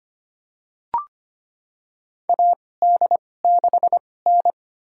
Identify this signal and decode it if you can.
Morse code, heard as a keyed tone, about 25 wpm, spaced as for 18 wpm, 700 Hz. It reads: AD6N
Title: USA Call Signs - 4 Characters - 2x Character Spacing - 25wpm